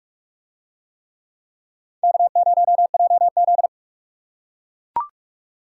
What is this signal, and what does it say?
Morse code, heard as a keyed tone, about 44 wpm, 700 Hz. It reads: K0JZ